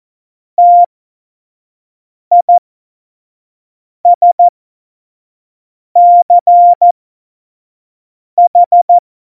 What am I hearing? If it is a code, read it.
Morse code, heard as a keyed tone, about 14 wpm, spaced as for 5 wpm, 700 Hz. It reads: TISCH
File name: Morsekurs / Witze / Witz 15. Oktober 2019, 25 F70